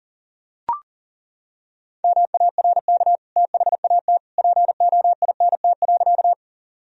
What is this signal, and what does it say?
Morse code, heard as a keyed tone, about 40 wpm, 700 Hz. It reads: MARK THAT POINT.
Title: Sentences from Top 1000 Words - Review Part 1 - 40wpm